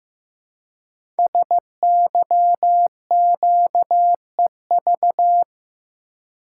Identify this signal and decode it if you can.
Morse code, heard as a keyed tone, about 15 wpm, 700 Hz. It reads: SYQEV
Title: Code Groups - 5 Letters - 15wpm